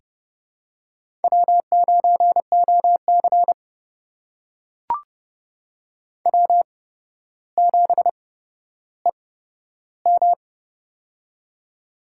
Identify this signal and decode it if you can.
Morse code, heard as a keyed tone, about 30 wpm, spaced as for 13 wpm, 700 Hz. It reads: W9OC W7EM
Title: USA Call Signs - 4 Characters - 8x Character Spacing - 30wpm